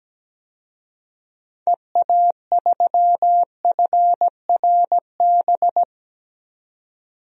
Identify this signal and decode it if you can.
Morse code, heard as a keyed tone, about 17 wpm, 700 Hz. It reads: EA3FRB
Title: DXCC-100 Call Signs - 6 Characters - 17wpm